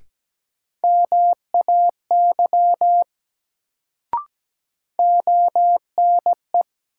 Morse code, heard as a keyed tone, about 17 wpm, 700 Hz. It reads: MAY ONE
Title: Sentences from Top 100 Words - New Word Drill - 17wpm